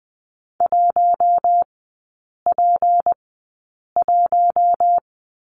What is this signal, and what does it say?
Morse code, heard as a keyed tone, about 20 wpm, spaced as for 8 wpm, 700 Hz. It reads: 1P1
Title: 198 Lesson 32e - Letter k - Three Characters 20wpm